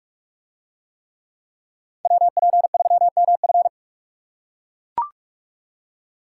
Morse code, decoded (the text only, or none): WP3MF